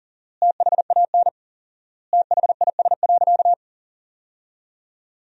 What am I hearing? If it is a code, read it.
Morse code, heard as a keyed tone, about 40 wpm, 700 Hz. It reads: THAN THIS.
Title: 4x Word-Spacing - Sentences from Top 100 Words - 40wpm